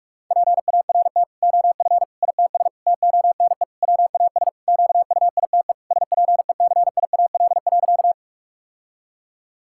Morse code, heard as a keyed tone, about 46 wpm, 700 Hz. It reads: PART OF ITS TONE WAS QUITE SPECIAL.
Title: Sentences from Top 1600 Words 45wpm